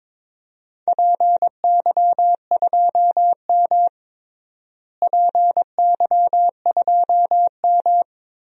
Morse code, heard as a keyed tone, about 22 wpm, 700 Hz. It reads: PY2M PY2M
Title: DXCC-100 Call Signs - 4 Characters - Repeated 2x - 22wpm